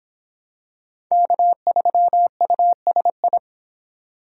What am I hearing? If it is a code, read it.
Morse code, heard as a keyed tone, about 26 wpm, 700 Hz. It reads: K3USI